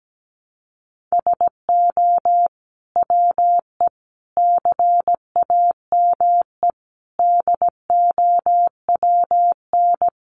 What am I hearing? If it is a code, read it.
Morse code, heard as a keyed tone, about 17 wpm, 700 Hz. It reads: SO WE CAME DOWN
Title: Sentences from Top 400 Words 17wpm